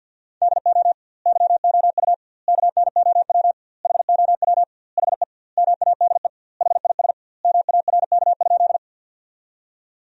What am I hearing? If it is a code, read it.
Morse code, heard as a keyed tone, about 48 wpm, 700 Hz. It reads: DO YOU KNOW HOW HE MADE HIS MARK?
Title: 2x Word-Spacing - Sentences from Top 300 Words - 50wpm